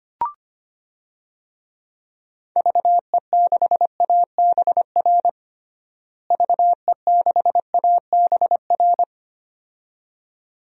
Morse code, heard as a keyed tone, about 25 wpm, 700 Hz. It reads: VE6ABR VE6ABR
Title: DXCC-100 Call Signs - 6 Characters - Repeated 2x - 25wpm